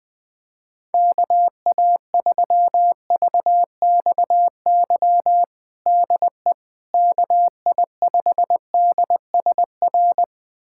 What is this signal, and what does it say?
Morse code, heard as a keyed tone, about 20 wpm, 700 Hz. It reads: KA3VXY DE KI5DSR
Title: USA Callsign DE Callsign 20wpm